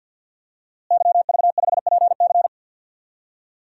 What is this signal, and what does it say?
Morse code, heard as a keyed tone, about 50 wpm, 700 Hz. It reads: YV5PC
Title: DXCC-100 Call Signs - 5 Characters - Repeated 3x - 50wpm